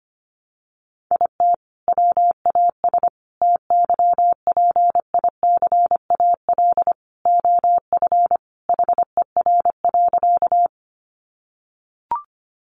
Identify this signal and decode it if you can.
Morse code, heard as a keyed tone, about 25 wpm, 700 Hz. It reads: IT WAS TYPICAL OF HER.
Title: Sentences from Top 1100 Words 25wpm